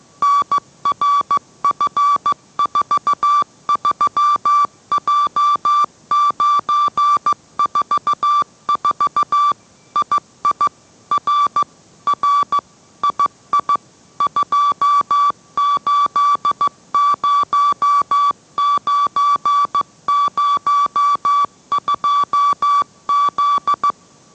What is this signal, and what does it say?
Morse code, heard as a keyed tone, about 15 wpm, 1150 Hz. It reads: NRF43J944 II R R II 280902Z